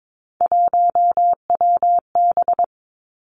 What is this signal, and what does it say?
Morse code, heard as a keyed tone, about 22 wpm, 700 Hz. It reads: 1WB
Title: Three Challenging Letter-Number Combinations 22wpm